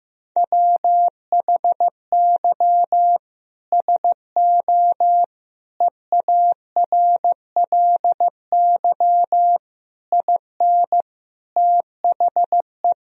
Morse code, heard as a keyed tone, about 15 wpm, 700 Hz. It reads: WHY SO EARLY IN THE
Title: Sentences from Top 1000 Words - Review Part 2 - 15wpm